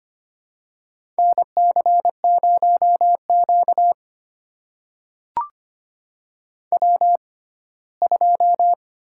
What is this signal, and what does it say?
Morse code, heard as a keyed tone, about 25 wpm, spaced as for 17 wpm, 700 Hz. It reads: NC0Q W2
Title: USA Call Signs - 4 Characters - 6x Character Spacing - 25wpm